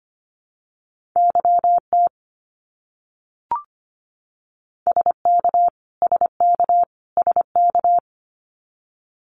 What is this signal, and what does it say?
Morse code, heard as a keyed tone, about 25 wpm, 700 Hz. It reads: YT SK SK SK
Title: Canadian Provinces and Territories - Repeated 3x - 25wpm